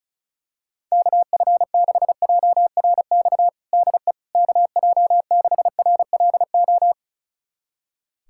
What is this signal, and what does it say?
Morse code, heard as a keyed tone, about 36 wpm, 700 Hz. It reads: KF6JRX DE KJ6RLO
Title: USA Callsign DE Callsign 35wpm